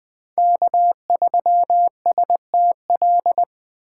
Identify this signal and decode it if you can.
Morse code, heard as a keyed tone, about 20 wpm, 700 Hz. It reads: K3STL
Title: 2023 POTA Hunter Call Signs - 20wpm